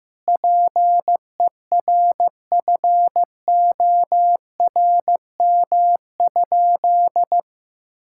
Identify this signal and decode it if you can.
Morse code, heard as a keyed tone, about 15 wpm, 700 Hz. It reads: PERFORM?
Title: Sentences from Top 1000 Words - Review Part 6 - 15wpm